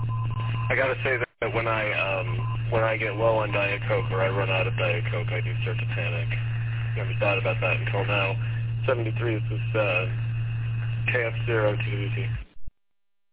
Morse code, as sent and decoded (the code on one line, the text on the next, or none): none